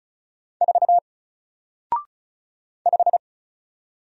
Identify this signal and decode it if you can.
Morse code, heard as a keyed tone, about 34 wpm, 700 Hz. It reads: 4 5
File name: Single-Letter-Number-35wpm